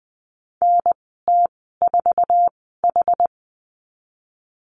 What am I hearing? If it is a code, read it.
Morse code, heard as a keyed tone, about 19 wpm, spaced as for 14 wpm, 700 Hz. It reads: NT4H
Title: 122 Lesson 20g - Number 3 - Call Signs 20wpm